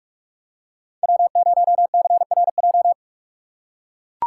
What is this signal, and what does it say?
Morse code, heard as a keyed tone, about 44 wpm, 700 Hz. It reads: W0CRJ
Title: USA Call Signs - 3x Character Spacing - 45wpm